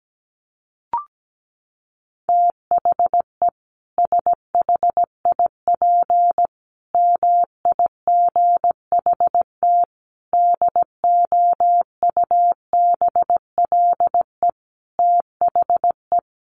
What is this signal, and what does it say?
Morse code, heard as a keyed tone, about 17 wpm, 700 Hz. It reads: THE SHIP MIGHT DOUBLE THE
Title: Sentences from Top 1400 Words 17wpm